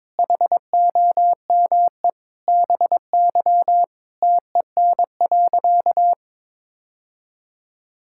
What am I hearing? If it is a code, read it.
Morse code, heard as a keyed tone, about 22 wpm, 700 Hz. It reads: HOME BY TEN.
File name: practice / 22wpm / Sentences with 4-Word Phrases - 22wpm